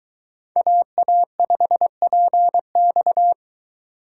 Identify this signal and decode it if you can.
Morse code, heard as a keyed tone, about 22 wpm, 700 Hz. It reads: AA5PX